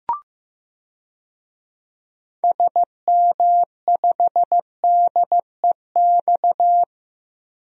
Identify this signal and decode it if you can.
Morse code, heard as a keyed tone, about 15 wpm, 700 Hz. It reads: SM5DEX
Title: DXCC-100 Call Signs - 6 Characters - 15wpm